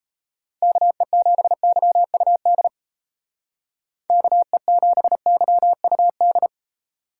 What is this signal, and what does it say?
Morse code, heard as a keyed tone, about 32 wpm, 700 Hz. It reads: KE7YUD KE7YUD
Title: Speed Racing - USA 2x3 Call Signs 28wpm